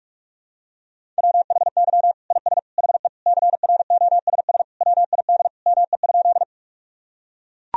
Morse code, heard as a keyed tone, about 41 wpm, 700 Hz. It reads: WHY IS HE CROSS WID ME?
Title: Sentences from Top 300 Words and Abbreviations 45wpm